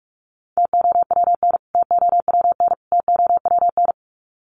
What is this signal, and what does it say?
Morse code, heard as a keyed tone, about 44 wpm, 700 Hz. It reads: TOWN TOWN TOWN
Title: Top 500 Words Repeated 45wpm